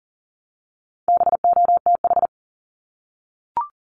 Morse code, heard as a keyed tone, about 40 wpm, 700 Hz. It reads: BOTH
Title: Top 500 Words Repeated 40wpm